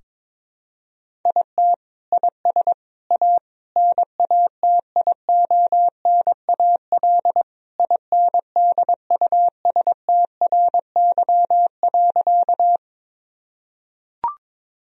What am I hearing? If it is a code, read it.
Morse code, heard as a keyed tone, about 22 wpm, 700 Hz. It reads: IT IS A NATIONAL INDUSTRY.
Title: Sentences from Top 1000 Words - Review Part 4 - 22wpm